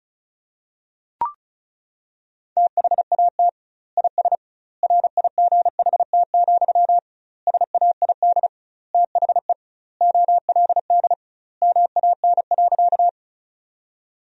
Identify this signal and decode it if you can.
Morse code, heard as a keyed tone, about 36 wpm, 700 Hz. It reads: THAT IS RIGHT, SAID THE OLD MAN.
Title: 2x Word-Spacing - Sentences from Top 200 Words - 35wpm